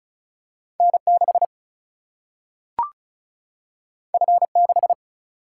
Morse code, heard as a keyed tone, about 36 wpm, 700 Hz. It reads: N 6 F 6
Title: Single-Letter Single-Number 35wpm